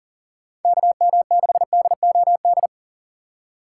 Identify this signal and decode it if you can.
Morse code, heard as a keyed tone, about 39 wpm, 700 Hz. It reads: KM6DOD
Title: USA Call Signs - 2x Character Spacing - 40wpm